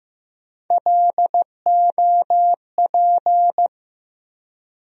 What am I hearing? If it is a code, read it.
Morse code, heard as a keyed tone, about 15 wpm, 700 Hz. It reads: LOP
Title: USA Call Sign Postfixes - 3 Letters 15wpm